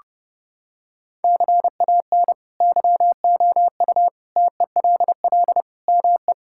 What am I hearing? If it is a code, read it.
Morse code, heard as a keyed tone, about 30 wpm, 700 Hz. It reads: CAN YOU TELL ME
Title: Sentences from Top 1600 Words 30wpm